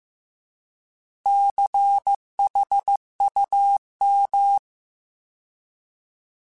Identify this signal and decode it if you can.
Morse code, heard as a keyed tone, about 15 wpm, 800 Hz. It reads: CHUM